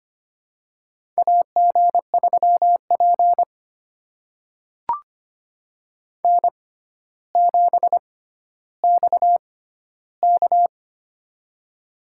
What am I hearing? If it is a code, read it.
Morse code, heard as a keyed tone, about 25 wpm, spaced as for 13 wpm, 700 Hz. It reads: AG3P N7XK
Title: USA Call Signs - 4 Characters - 6x Character Spacing - 25wpm